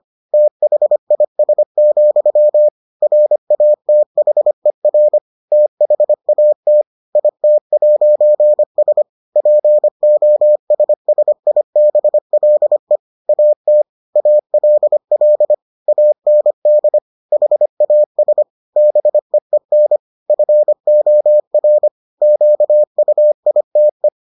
Morse code, heard as a keyed tone, about 25 wpm, 600 Hz. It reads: THIS, RATHER THAT IT'S POSSIBLE AT ALL AND HAS BEEN FOR QUITE